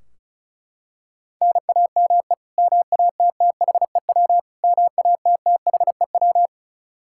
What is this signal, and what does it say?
Morse code, heard as a keyed tone, about 36 wpm, 700 Hz. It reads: NAME MATTHEW MATTHEW
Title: K1USN SST QSO Practice 35wpm